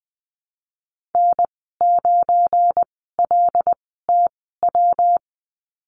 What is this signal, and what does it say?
Morse code, heard as a keyed tone, about 20 wpm, spaced as for 15 wpm, 700 Hz. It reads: N9LTW
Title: 179 Lesson 29g - Slash - Call Signs 20wpm